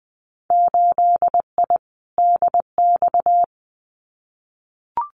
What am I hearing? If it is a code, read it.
Morse code, heard as a keyed tone, about 19 wpm, 700 Hz. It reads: 8I DX
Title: Winter Field Day Exchanges Encore 20wpm